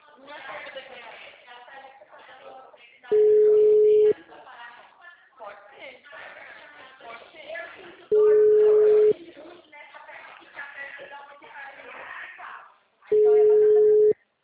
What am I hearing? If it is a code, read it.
Ringback tone (Europe, 425 Hz), with speech.